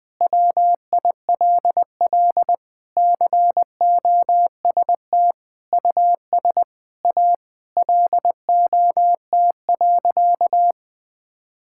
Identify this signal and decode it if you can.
Morse code, heard as a keyed tone, about 19 wpm, 700 Hz. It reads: WILL COST US A LOT.
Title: Sentences from Top 1000 Words - Review Part 5 - 20wpm